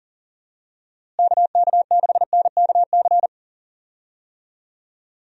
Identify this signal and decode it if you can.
Morse code, heard as a keyed tone, about 40 wpm, 700 Hz. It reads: KK6NKC
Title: USA Call Signs - 6 Characters - Repeated 3x - 40wpm